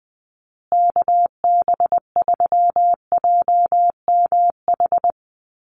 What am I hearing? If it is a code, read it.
Morse code, heard as a keyed tone, about 20 wpm, 700 Hz. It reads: KB3JMH
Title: USA Call Signs Repeated 3x 20wpm